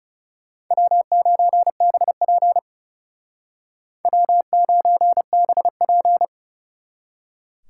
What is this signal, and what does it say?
Morse code, heard as a keyed tone, about 39 wpm, 700 Hz. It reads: W9BP W9BP